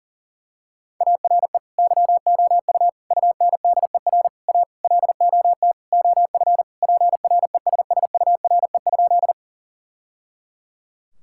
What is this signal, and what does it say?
Morse code, heard as a keyed tone, about 40 wpm, 700 Hz. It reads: ARE YOU UNDER A LOT OF PRESSURE?
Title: Sentences from Top 1000 Words - Review Part 6 - 40wpm